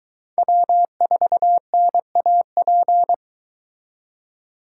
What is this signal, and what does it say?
Morse code, heard as a keyed tone, about 24 wpm, 700 Hz. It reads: W4NAP